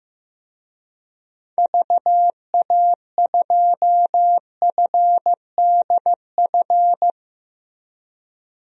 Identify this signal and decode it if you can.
Morse code, heard as a keyed tone, about 15 wpm, 700 Hz. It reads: VA2FDF
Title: DXCC-100 Call Signs - 6 Characters - Repeated 2x - 15wpm